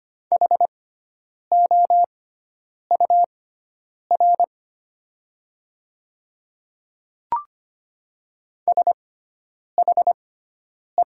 Morse code, heard as a keyed tone, about 25 wpm, spaced as for 9 wpm, 700 Hz. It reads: HOUR SHE
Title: 6x Character Spacing - Top 500 Words - 25wpm